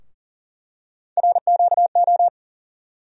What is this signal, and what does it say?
Morse code, heard as a keyed tone, about 40 wpm, 700 Hz. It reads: RQO
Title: USA Call Sign Postfixes - 3 Letters 40wpm